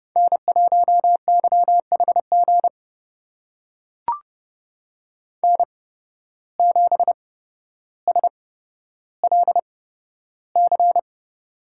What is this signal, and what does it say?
Morse code, heard as a keyed tone, about 30 wpm, spaced as for 13 wpm, 700 Hz. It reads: N1YHG N7SLC
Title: USA Call Signs - 5 Characters - 8x Character Spacing - 30wpm